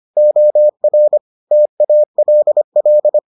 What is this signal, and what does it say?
Morse code, heard as a keyed tone, about 25 wpm, 600 Hz. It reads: OR TALL